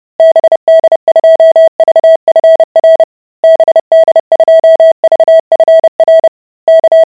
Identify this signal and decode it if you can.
Morse code, heard as a keyed tone, about 30 wpm, 650 Hz. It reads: BD2VFR BD2VFR K